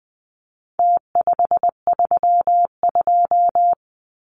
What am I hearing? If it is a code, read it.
Morse code, heard as a keyed tone, about 20 wpm, 700 Hz. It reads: T532
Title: Numbers - Contest Style - 4-Digit 20wpm